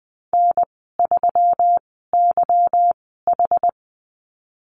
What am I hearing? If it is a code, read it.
Morse code, heard as a keyed tone, about 20 wpm, spaced as for 15 wpm, 700 Hz. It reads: N3YH
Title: 167 Lesson 27g - Number 7 - Call Signs 20wpm